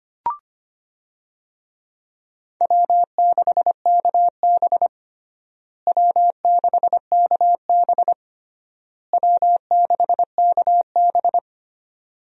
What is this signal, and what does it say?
Morse code, heard as a keyed tone, about 25 wpm, 700 Hz. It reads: W6KB W6KB W6KB
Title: USA Call Signs - 4 Characters - Repeated 3x - 25wpm